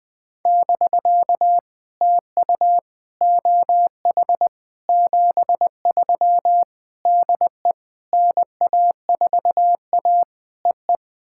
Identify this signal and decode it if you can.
Morse code, heard as a keyed tone, about 20 wpm, 700 Hz. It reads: <BK> TU OH 73 DE NA4A EE
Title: POTA QSO Practice - 20wpm